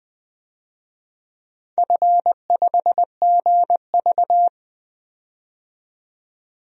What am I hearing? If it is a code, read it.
Morse code, heard as a keyed tone, about 20 wpm, 700 Hz. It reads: F5GV
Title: DXCC-100 Call Signs - 4 Characters - 20wpm